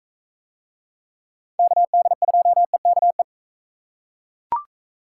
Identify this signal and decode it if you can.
Morse code, heard as a keyed tone, about 41 wpm, 700 Hz. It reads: KD2EKE